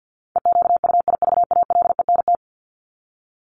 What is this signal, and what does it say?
Morse code, heard as a keyed tone, about 50 wpm, 700 Hz. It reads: EQUIVALENT
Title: Top 1501-1750 Words 50wpm